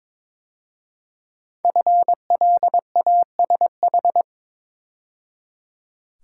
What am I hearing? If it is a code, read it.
Morse code, heard as a keyed tone, about 23 wpm, 700 Hz. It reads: FLASH
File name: 5 Letter Words - First 3 Letters Given - 22wpm